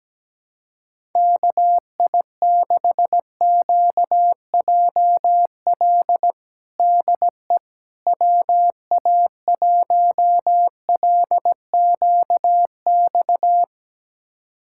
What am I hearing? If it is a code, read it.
Morse code, heard as a keyed tone, about 17 wpm, 700 Hz. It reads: KI6QJL DE WA1LQX
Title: USA Callsign DE Callsign - 6 Characters - 17wpm